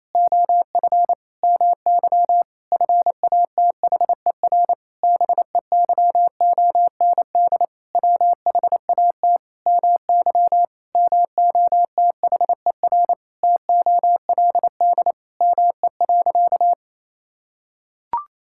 Morse code, heard as a keyed tone, about 28 wpm, 700 Hz. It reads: OF MY FATHER BEYOND WHAT MY MOTHER TOLD ME.
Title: Sentences from Top 1000 Words - Review Part 6 - 28wpm